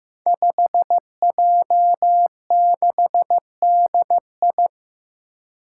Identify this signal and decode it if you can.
Morse code, heard as a keyed tone, about 15 wpm, 700 Hz. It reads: 5J6DI
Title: Five Challenging Letter-Number Combinations 15wpm